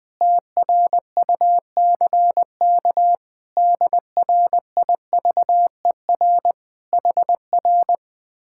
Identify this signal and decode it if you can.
Morse code, heard as a keyed tone, about 20 wpm, 700 Hz. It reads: TRUCK DRIVER HR